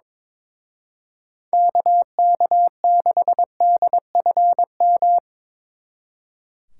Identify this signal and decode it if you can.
Morse code, heard as a keyed tone, about 25 wpm, 700 Hz. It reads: KK6DFM
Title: Speed Racing - USA 2x3 Call Signs 22wpm